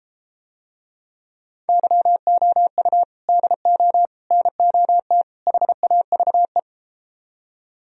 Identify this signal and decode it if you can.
Morse code, heard as a keyed tone, about 33 wpm, 700 Hz. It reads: YOU DO NOT HAVE